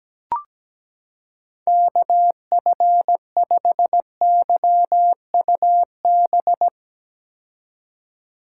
Morse code, heard as a keyed tone, about 17 wpm, 700 Hz. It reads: KF5YUB
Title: K1USN SST QSO Practice - 17wpm